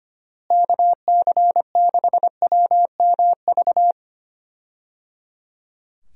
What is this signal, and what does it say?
Morse code, heard as a keyed tone, about 25 wpm, 700 Hz. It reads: KC6WMV
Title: USA Call Signs - 6 Characters - Repeated 3x - 25wpm